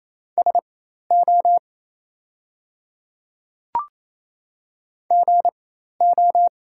Morse code, heard as a keyed tone, about 27 wpm, spaced as for 13 wpm, 690 Hz. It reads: SO GO
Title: 4x Character Spacing - Top 100 Words - 28wpm